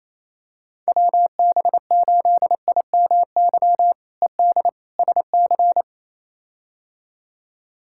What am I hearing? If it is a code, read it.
Morse code, heard as a keyed tone, about 28 wpm, 700 Hz. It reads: WB8IMY ED SC
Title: K1USN SST QSO Practice - 28wpm